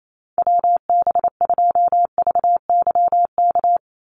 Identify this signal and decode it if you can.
Morse code, heard as a keyed tone, about 28 wpm, 700 Hz. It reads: WB2VYK
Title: Call Signs Repeated 2x - 28wpm